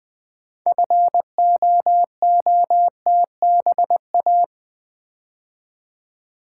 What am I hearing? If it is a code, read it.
Morse code, heard as a keyed tone, about 20 wpm, 700 Hz. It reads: FOOTBA
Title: Build A Word - 3 Letters At A Time - 8-Letter Words - 20wpm